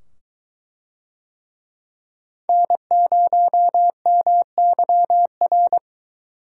Morse code, heard as a keyed tone, about 22 wpm, 700 Hz. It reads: N0MYR